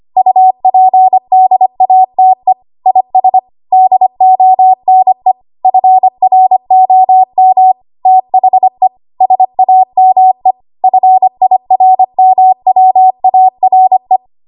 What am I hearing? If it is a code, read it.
Morse code, heard as a keyed tone, about 25 wpm, 750 Hz. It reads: UPDATE IS DONE FROM THE SAME FIRMWARE
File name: Archive / 25 WPM / 25WPM